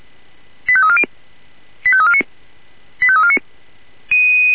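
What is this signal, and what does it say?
Five-tone selective call (ZVEI), heard as quick runs of five tones, tones of about 70 ms. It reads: 85340 85349 85349
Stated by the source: Five-tone ZVEI selective call received by SDR.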